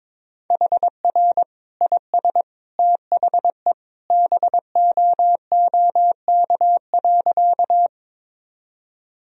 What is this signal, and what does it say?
Morse code, heard as a keyed tone, about 22 wpm, 700 Hz. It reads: HR IS THE BOOK.
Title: Sentences from Top 300 Words and Abbreviations 22wpm